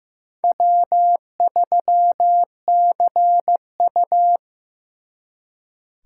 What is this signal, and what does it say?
Morse code, heard as a keyed tone, about 18 wpm, 700 Hz. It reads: W3CU
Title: Speed Racing - USA 1x2 Call Signs 15wpm